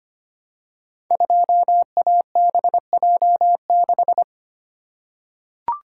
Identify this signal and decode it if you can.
Morse code, heard as a keyed tone, about 25 wpm, 700 Hz. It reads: 2ABJ6
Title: Code Groups - 5 Letter:Numbers - 25wpm